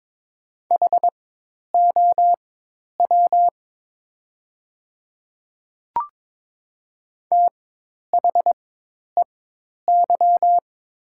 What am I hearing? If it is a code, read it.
Morse code, heard as a keyed tone, about 22 wpm, spaced as for 10 wpm, 700 Hz. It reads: HOW THEY
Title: 4x Character Spacing - Top 100 Words - 22wpm